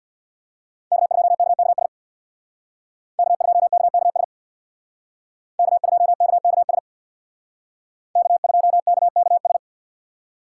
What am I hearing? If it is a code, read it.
Morse code, heard as a keyed tone, about 57 wpm, 700 Hz. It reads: K2KKS K2KKS K2KKS K2KKS
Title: Speed Racing - USA 1x3 Call Signs 50wpm